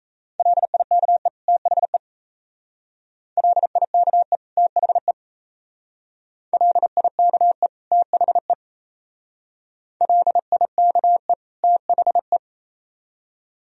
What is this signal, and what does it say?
Morse code, heard as a keyed tone, about 35 wpm, 690 Hz. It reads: LIKE THE LIKE THE LIKE THE LIKE THE
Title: Speed-Racing - Sets of 2 Words 28wpm